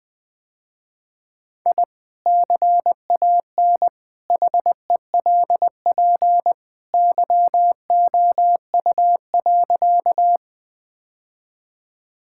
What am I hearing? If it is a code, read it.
Morse code, heard as a keyed tone, about 20 wpm, 700 Hz. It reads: I CAN HELP YOU.
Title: Sentences with 2-Word Phrases - Spoken Beginning - 20wpm